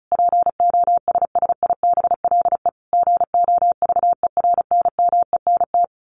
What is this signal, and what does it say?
Morse code, heard as a keyed tone, about 34 wpm, 700 Hz. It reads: POSSIBLE GOVERNMENT
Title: A Sound Of Thunder 35wpm